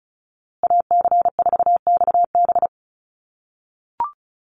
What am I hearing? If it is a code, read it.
Morse code, heard as a keyed tone, about 34 wpm, 700 Hz. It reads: AC4XB
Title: USA Call Signs Repeated 2x 35wpm